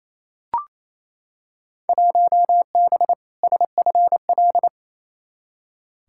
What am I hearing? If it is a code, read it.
Morse code, heard as a keyed tone, about 28 wpm, 700 Hz. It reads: 1B SFL
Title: Field Day Exchanges Encore - 28wpm